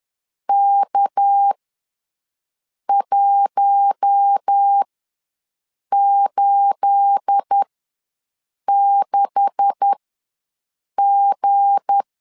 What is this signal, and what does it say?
Morse code, heard as a keyed tone, about 10 wpm, spaced as for 6 wpm, 800 Hz. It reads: K186G
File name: LEZIONI / 9 286 ws 10wpm 4032